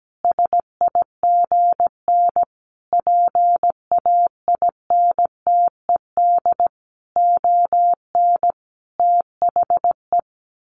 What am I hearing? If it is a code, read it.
Morse code, heard as a keyed tone, about 17 wpm, 700 Hz. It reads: SIGN PAINTED ON THE
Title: A Sound Of Thunder 17wpm